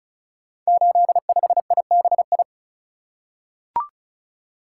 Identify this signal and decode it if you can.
Morse code, heard as a keyed tone, about 36 wpm, 700 Hz. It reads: 85IBI
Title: Code Groups - 5 Letter:Numbers - 35wpm